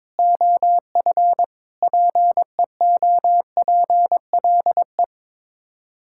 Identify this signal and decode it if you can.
Morse code, heard as a keyed tone, about 21 wpm, 700 Hz. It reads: OF PEOPLE
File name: practice / 22wpm / Build A Phrase - 5-Word Phrases - 22wpm